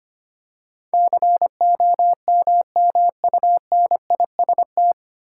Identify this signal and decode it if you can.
Morse code, heard as a keyed tone, about 25 wpm, 700 Hz. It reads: COMMUNIST
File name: practice / 25wpm / 9 Letter Words - First 3 Letters Given - 25wpm